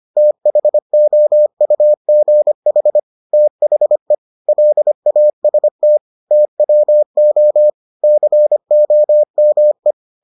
Morse code, heard as a keyed tone, about 25 wpm, 600 Hz. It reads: THOUGH THE LAST TWO COME